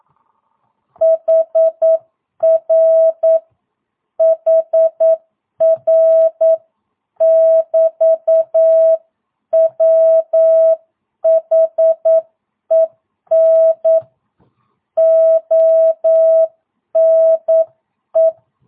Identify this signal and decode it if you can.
Morse code, heard as a keyed tone, about 9 wpm, 650 Hz. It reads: HR HR <BT> WHEN ONE
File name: jh8cbh / 2022cw